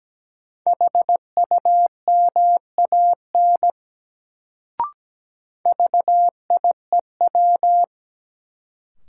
Morse code, heard as a keyed tone, about 17 wpm, 700 Hz. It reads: HUMAN VIEW
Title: Sets of 5 Words - New Word Drill - 17wpm